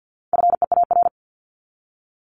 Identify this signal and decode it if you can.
Morse code, heard as a keyed tone, about 48 wpm, 700 Hz. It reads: FEAR
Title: Top 1751-2000 Words 50wpm